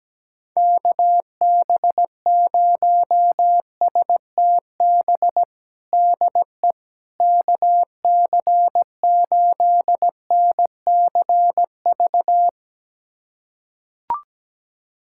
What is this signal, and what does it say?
Morse code, heard as a keyed tone, about 17 wpm, 700 Hz. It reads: KB0STB DE KC8NCV
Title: USA Callsign DE Callsign - 6 Characters - 17wpm